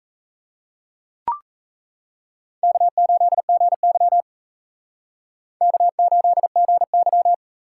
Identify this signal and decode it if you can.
Morse code, heard as a keyed tone, about 35 wpm, 700 Hz. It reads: K8GY K8GY
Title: Speed Racing - USA 1x2 Call Signs 28wpm